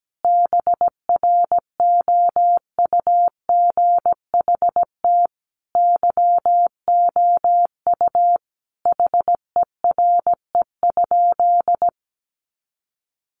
Morse code, heard as a keyed tone, about 17 wpm, 700 Hz. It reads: BROUGHT YOU HERE?
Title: Sentences from Top 500 Words 17wpm